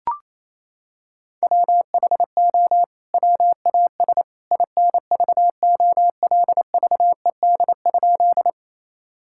Morse code, heard as a keyed tone, about 28 wpm, 700 Hz. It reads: WHO WAS INVOLVED?
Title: Sentences from Top 1000 Words - Review Part 5 - 28wpm